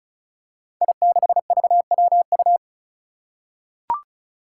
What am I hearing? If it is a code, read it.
Morse code, heard as a keyed tone, about 36 wpm, 700 Hz. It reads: I6VWU
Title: Five Challenging Letter-Number Combinations 35wpm